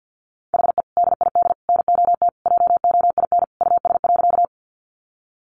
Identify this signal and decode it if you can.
Morse code, heard as a keyed tone, about 50 wpm, 700 Hz. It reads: HE DID NOT JOIN US.
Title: Sentences from Top 1400 Words 50wpm